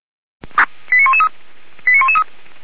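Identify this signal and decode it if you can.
Five-tone selective call (ZVEI), heard as quick runs of five tones, tones of about 70 ms. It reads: 89113 89113
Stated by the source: Received by SDR on a ZVEI selective-call channel, archived by Feuerwehr Orlamünde.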